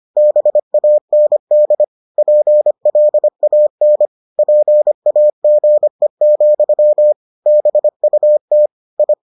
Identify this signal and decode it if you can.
Morse code, heard as a keyed tone, about 25 wpm, 600 Hz. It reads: BAND PLAN PAGE, BUT I